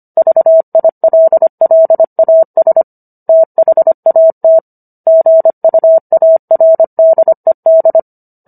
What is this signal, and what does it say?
Morse code, heard as a keyed tone, about 25 wpm, 650 Hz. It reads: VILLAS THAT GUARDED